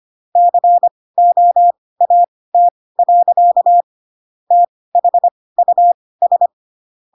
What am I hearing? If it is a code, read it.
Morse code, heard as a keyed tone, about 25 wpm, spaced as for 18 wpm, 700 Hz. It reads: COAT. THUS
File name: Pinocchio / Pinocchio ch2 12 25 ik1hge